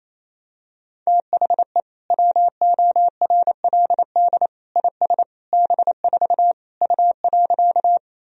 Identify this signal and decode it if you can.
Morse code, heard as a keyed tone, about 28 wpm, 700 Hz. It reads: THE WORLD IS B4 U.
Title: Sentences from Top 200 Words and Abbreviations 28wpm